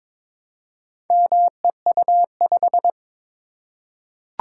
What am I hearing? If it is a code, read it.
Morse code, heard as a keyed tone, about 22 wpm, 700 Hz. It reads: MEU5
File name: practice / 22wpm / Four Letter-Number Combinations - 22wpm